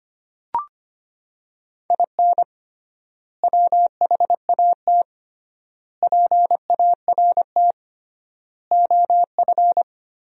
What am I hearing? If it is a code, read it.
Morse code, heard as a keyed tone, about 25 wpm, 700 Hz. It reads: IN WHAT PART OF
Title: 3x Word-Spacing - Sentences from Top 200 Words - 25wpm